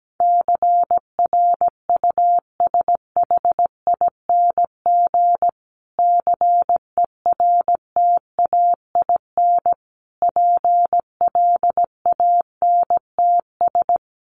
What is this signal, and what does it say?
Morse code, heard as a keyed tone, about 17 wpm, 700 Hz. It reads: CRUSHING CERTAIN PLANTS
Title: A Sound Of Thunder 17wpm